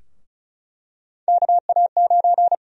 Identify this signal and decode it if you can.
Morse code, heard as a keyed tone, about 36 wpm, 700 Hz. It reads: KA9
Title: USA Call Sign Prefixes - Repeated 2x - 35wpm